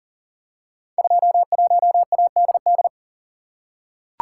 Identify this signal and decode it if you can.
Morse code, heard as a keyed tone, about 38 wpm, 700 Hz. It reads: 21ADD
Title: Code Groups - 5 Letter:Numbers - 40wpm